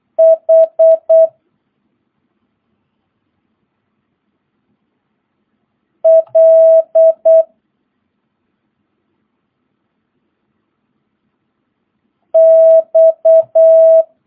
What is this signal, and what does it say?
Morse code, heard as a keyed tone, about 8 wpm, 650 Hz. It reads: H L X
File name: jh8cbh / 2022cw